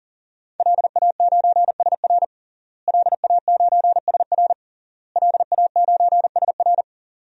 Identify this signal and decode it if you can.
Morse code, heard as a keyed tone, about 40 wpm, 700 Hz. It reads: LA9SR LA9SR LA9SR
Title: DXCC-100 Call Signs - 5 Characters - Repeated 3x - 40wpm